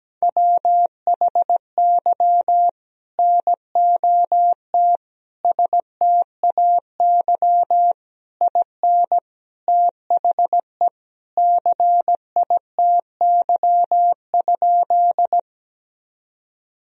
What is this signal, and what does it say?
Morse code, heard as a keyed tone, about 17 wpm, 700 Hz. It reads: WHY NOT STAY IN THE CITY?
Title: Sentences from Top 1000 Words - Review Part 1 - 17wpm